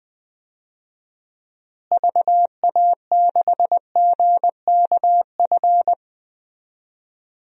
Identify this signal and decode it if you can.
Morse code, heard as a keyed tone, about 19 wpm, 700 Hz. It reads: VA6GKF
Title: DXCC-100 Call Signs - 6 Characters - Repeated 3x - 20wpm